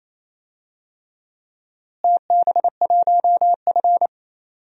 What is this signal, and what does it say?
Morse code, heard as a keyed tone, about 28 wpm, 700 Hz. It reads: TB1F
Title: DXCC-100 Call Signs - 4 Characters - Repeated 3x - 28wpm